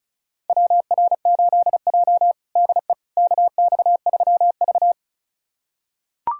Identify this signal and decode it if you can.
Morse code, heard as a keyed tone, about 34 wpm, 700 Hz. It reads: WR8J DE KX3V
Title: USA Callsign DE Callsign - 4 Characters - 35wpm